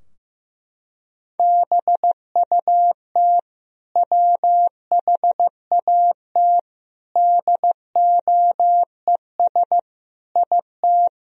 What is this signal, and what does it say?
Morse code, heard as a keyed tone, about 15 wpm, 700 Hz. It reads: BUT WHAT DOES IT
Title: Sentences from Top 1000 Words - Review Part 4 - 15wpm